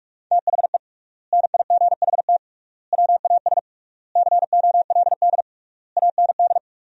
Morse code, heard as a keyed tone, about 46 wpm, 700 Hz. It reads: THE NIGHT WAS COLD AND
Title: 3x Word-Spacing - Sentences from Top 400 Words - 45wpm